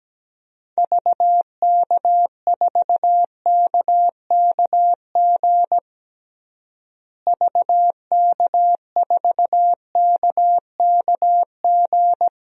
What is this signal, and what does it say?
Morse code, heard as a keyed tone, about 17 wpm, 700 Hz. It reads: VK4KKG VK4KKG
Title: DXCC-100 Call Signs - 6 Characters - Repeated 2x - 17wpm